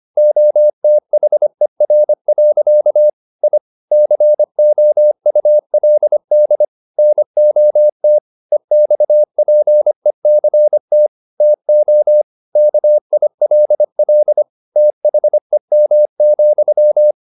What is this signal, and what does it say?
Morse code, heard as a keyed tone, about 25 wpm, 600 Hz. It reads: OTHER. I COULD NOT EXPECT TO KILL THEM,